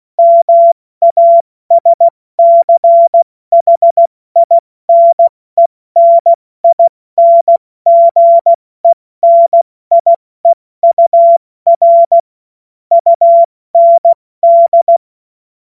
Morse code, heard as a keyed tone, about 16 wpm, 680 Hz. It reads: MASCHINENINGENIEUR UND
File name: Morsekurs / Witze / Witz 14. Dezember 2021, 70 F80